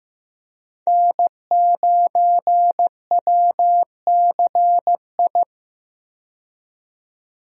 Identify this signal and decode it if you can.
Morse code, heard as a keyed tone, about 15 wpm, 700 Hz. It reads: N9WCI
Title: USA Call Signs - 5 Characters - 15wpm